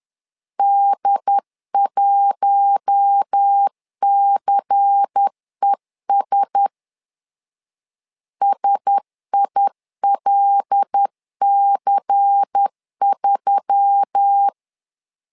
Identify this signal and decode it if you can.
Morse code, heard as a keyed tone, about 10 wpm, 800 Hz. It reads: D1CES SILC3
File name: LEZIONI / 5 LBVC / 5 LBVC 10wpm 2349